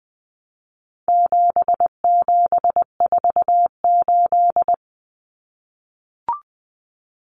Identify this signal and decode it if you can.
Morse code, heard as a keyed tone, about 20 wpm, 700 Hz. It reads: 7748
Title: Common Cut Numbers - 4-Digit 20wpm